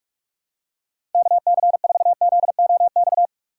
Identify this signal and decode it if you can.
Morse code, heard as a keyed tone, about 44 wpm, 700 Hz. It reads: KC4ZOX
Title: USA Call Signs - 6 Characters - 3x Character Spacing - 45wpm